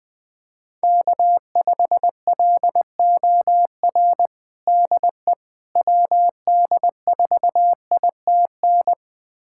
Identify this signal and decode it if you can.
Morse code, heard as a keyed tone, about 20 wpm, 700 Hz. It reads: K5LOR DE WD4ITN
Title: USA Callsign DE Callsign 20wpm